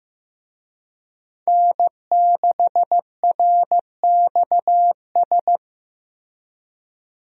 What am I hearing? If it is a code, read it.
Morse code, heard as a keyed tone, about 15 wpm, 700 Hz. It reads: N6RXS
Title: USA Call Signs - 5 Characters - Repeated 3x - 15wpm